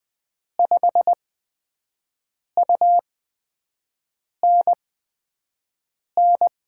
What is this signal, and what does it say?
Morse code, heard as a keyed tone, about 20 wpm, spaced as for 5 wpm, 700 Hz. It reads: 5UNN
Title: USA Call Signs - 5 Characters - 8x Character Spacing - 20wpm